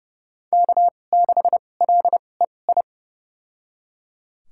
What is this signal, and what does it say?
Morse code, heard as a keyed tone, about 30 wpm, spaced as for 22 wpm, 700 Hz. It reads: K6LEI
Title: USA Call Signs - 2x Character Spacing - 30wpm